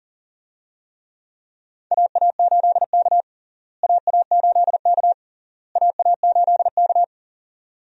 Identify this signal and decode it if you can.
Morse code, heard as a keyed tone, about 40 wpm, 700 Hz. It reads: AA8K AA8K AA8K
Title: USA Call Signs - 4 Characters - Repeated 3x - 40wpm